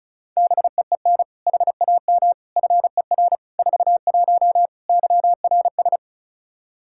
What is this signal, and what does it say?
Morse code, heard as a keyed tone, about 34 wpm, 700 Hz. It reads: BEEN HAM FER 41 YRS
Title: Practice QSO - Exchange 3 - More Info - 35wpm